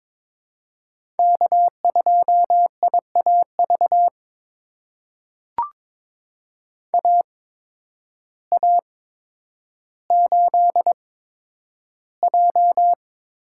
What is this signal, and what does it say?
Morse code, heard as a keyed tone, about 22 wpm, spaced as for 10 wpm, 700 Hz. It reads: K2IAV AA8J
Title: USA Call Signs - 5 Characters - 8x Character Spacing - 22wpm